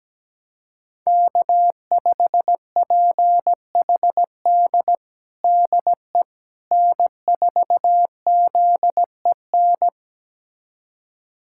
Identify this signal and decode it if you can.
Morse code, heard as a keyed tone, about 17 wpm, 700 Hz. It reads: K5PHD DE N4ZEN
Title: USA Callsign DE Callsign - 5 Characters - 17wpm